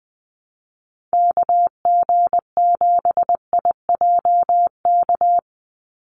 Morse code, heard as a keyed tone, about 19 wpm, 700 Hz. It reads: KG7IJK
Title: USA Call Signs 20wpm